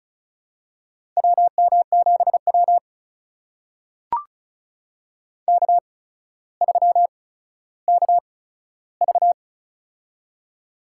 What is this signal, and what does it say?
Morse code, heard as a keyed tone, about 36 wpm, spaced as for 15 wpm, 700 Hz. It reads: WM7W K3KV
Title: USA Call Signs - 4 Characters - 8x Character Spacing - 35wpm